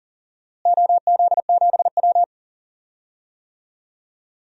Morse code, heard as a keyed tone, about 40 wpm, 700 Hz. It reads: OZ7W
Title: DXCC-100 Call Signs - 4 Characters - Repeated 2x - 40wpm